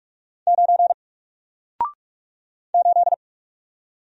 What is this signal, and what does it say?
Morse code, heard as a keyed tone, about 44 wpm, 700 Hz. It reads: 9 8
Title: Single Letter-Number Alt 45wpm